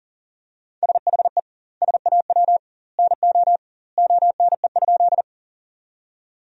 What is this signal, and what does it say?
Morse code, heard as a keyed tone, about 40 wpm, 700 Hz. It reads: SHE SAW NO ONE?
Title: 2x Word-Spacing - Sentences from Top 300 Words - 40wpm